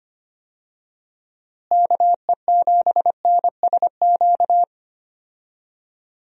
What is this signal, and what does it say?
Morse code, heard as a keyed tone, about 25 wpm, 700 Hz. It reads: KE7NSQ
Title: USA Call Signs - 6 Characters - Repeated 3x - 25wpm